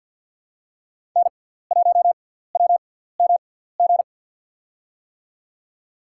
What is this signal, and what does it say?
Morse code, heard as a keyed tone, about 53 wpm, 700 Hz. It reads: N 1 W M G
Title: USA Call Signs - 5 Characters - 6x Character Spacing - 50wpm